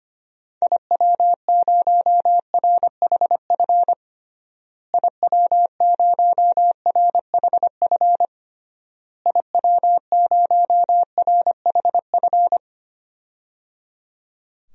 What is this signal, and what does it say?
Morse code, heard as a keyed tone, about 25 wpm, 700 Hz. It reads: IW0RHF IW0RHF IW0RHF
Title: DXCC-100 Call Signs - 6 Characters - Repeated 3x - 25wpm